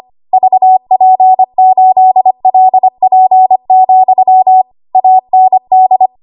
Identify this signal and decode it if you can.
Morse code, heard as a keyed tone, about 25 wpm, 750 Hz. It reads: VP8LP, AND
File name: Morse / 25WPM